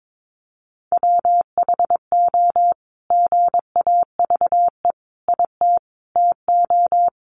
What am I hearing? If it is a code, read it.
Morse code, heard as a keyed tone, about 22 wpm, 700 Hz. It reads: WHO GAVE IT TO
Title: Sentences from Top 500 Words 22wpm